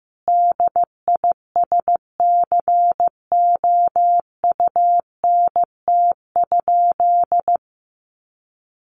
Morse code, heard as a keyed tone, about 15 wpm, 700 Hz. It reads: DISCOUNT?
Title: Sentences from Top 1200 Words 15wpm